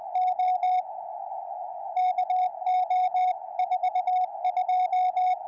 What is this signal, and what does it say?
Morse code, heard as a keyed tone, about 20 wpm, 740 Hz. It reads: O KO 42
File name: tropo / ew8w tr